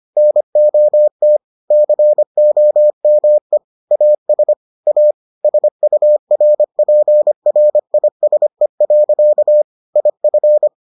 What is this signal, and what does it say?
Morse code, heard as a keyed tone, about 25 wpm, 600 Hz. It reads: NOT COME AS A SURPRISE. IF